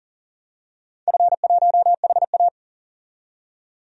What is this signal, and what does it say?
Morse code, heard as a keyed tone, about 40 wpm, 700 Hz. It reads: F1HA
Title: DXCC-100 Call Signs - 4 Characters - 40wpm